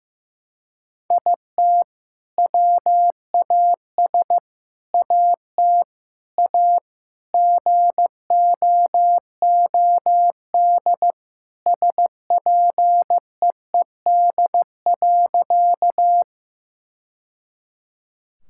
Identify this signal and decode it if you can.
Morse code, heard as a keyed tone, about 14 wpm, 700 Hz. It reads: IT WAS AT A GOOD SPEED.
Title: Sentences from Top 1000 Words - Review Part 5 - 15wpm